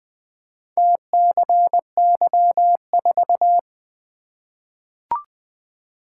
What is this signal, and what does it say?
Morse code, heard as a keyed tone, about 20 wpm, 700 Hz. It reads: TCY4
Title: Four Letter-Number Combinations - 20wpm